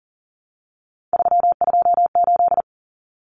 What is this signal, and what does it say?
Morse code, heard as a keyed tone, about 40 wpm, 700 Hz. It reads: 328
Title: Common Cut Numbers - 3-Digit 40wpm